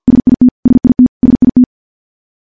Morse code, heard as a keyed tone, about 50 wpm, 250 Hz. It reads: SIT SIT SIT
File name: morse-training-50WPM